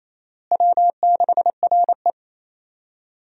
Morse code, heard as a keyed tone, about 28 wpm, 700 Hz. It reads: W6RE
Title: USA Call Signs - 4 Characters - Repeated 3x - 28wpm